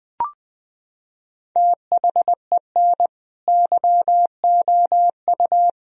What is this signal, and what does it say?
Morse code, heard as a keyed tone, about 20 wpm, 700 Hz. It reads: THEN YOU
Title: Sentences from Top 1000 Words - Review Part 1 - 20wpm